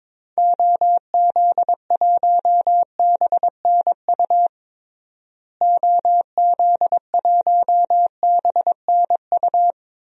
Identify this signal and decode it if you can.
Morse code, heard as a keyed tone, about 22 wpm, 700 Hz. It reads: OZ1BNU OZ1BNU
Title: DXCC-100 Call Signs - 6 Characters - Repeated 2x - 22wpm